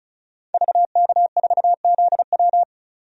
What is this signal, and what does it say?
Morse code, heard as a keyed tone, about 34 wpm, 700 Hz. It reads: VK4ZW
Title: DXCC-100 Call Signs - 5 Characters - 35wpm